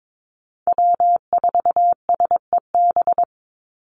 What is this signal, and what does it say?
Morse code, heard as a keyed tone, about 22 wpm, 700 Hz. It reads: W4SEB
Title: USA Call Signs 22wpm